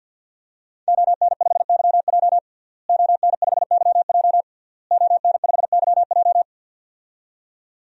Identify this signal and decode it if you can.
Morse code, heard as a keyed tone, about 52 wpm, 700 Hz. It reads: ON5YJ ON5YJ ON5YJ
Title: DXCC-100 Call Signs - 5 Characters - Repeated 3x - 50wpm